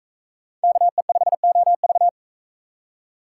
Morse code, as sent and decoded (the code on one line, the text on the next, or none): -.- . ..... --- ...-
KE5OV